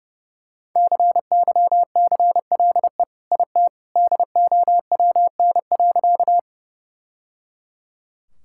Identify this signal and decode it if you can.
Morse code, heard as a keyed tone, about 30 wpm, 700 Hz. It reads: CYCLE IT DOWN.
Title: Sentences from Top 1000 Words - Review Part 6 - 30wpm